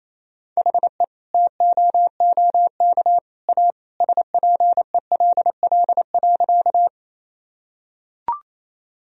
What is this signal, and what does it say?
Morse code, heard as a keyed tone, about 27 wpm, 700 Hz. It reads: HE TOOK A SPELL.
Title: Sentences from Top 1000 Words - Review Part 1 - 28wpm